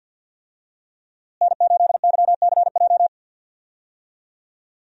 Morse code, heard as a keyed tone, about 53 wpm, 700 Hz. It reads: N8YCJ
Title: USA Call Signs - 5 Characters - Repeated 3x - 50wpm